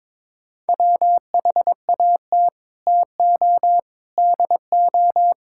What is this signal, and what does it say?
Morse code, heard as a keyed tone, about 22 wpm, 700 Hz. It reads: WHAT TO DO
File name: practice / 22wpm / Sets of 8 Words - 22wpm